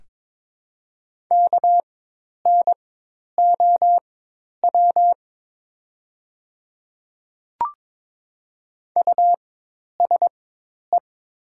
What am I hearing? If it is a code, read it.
Morse code, heard as a keyed tone, about 23 wpm, spaced as for 10 wpm, 700 Hz. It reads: KNOW USE
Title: 4x Character Spacing - Top 100 Words - 22wpm